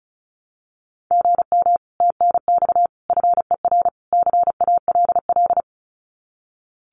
Morse code, heard as a keyed tone, about 36 wpm, 700 Hz. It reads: GM TNX FER CALL
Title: Common QSO Phrases Encore 35wpm